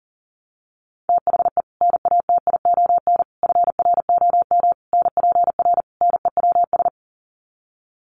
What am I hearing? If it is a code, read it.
Morse code, heard as a keyed tone, about 40 wpm, 700 Hz. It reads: THE NATION FROM NPR NEWS
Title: Sets of 5 Words 40wpm